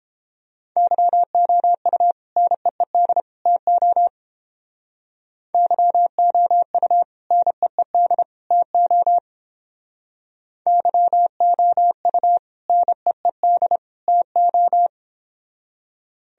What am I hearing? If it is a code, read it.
Morse code, heard as a keyed tone, about 30 wpm, 700 Hz. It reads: YOU NEED TO YOU NEED TO YOU NEED TO